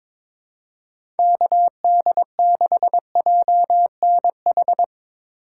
Morse code, heard as a keyed tone, about 22 wpm, 700 Hz. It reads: KD6JNH
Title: USA Call Signs - 2x Character Spacing - 22wpm